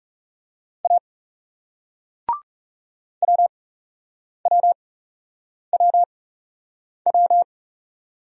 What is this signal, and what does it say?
Morse code, heard as a keyed tone, about 39 wpm, spaced as for 8 wpm, 700 Hz. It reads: A WWWW